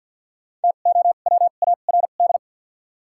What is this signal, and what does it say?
Morse code, heard as a keyed tone, about 50 wpm, spaced as for 36 wpm, 700 Hz. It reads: TOWARD
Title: 2x Character Spacing - Top 500 Words - 50wpm